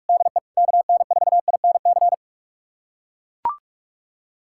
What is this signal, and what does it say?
Morse code, heard as a keyed tone, about 42 wpm, 700 Hz. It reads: DE KN4INC
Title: USA Callsign DE Callsign - 6 Characters - 45wpm